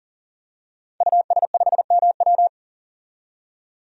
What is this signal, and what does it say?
Morse code, heard as a keyed tone, about 40 wpm, 700 Hz. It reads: US5MW
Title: DXCC-100 Call Signs - 5 Characters - 40wpm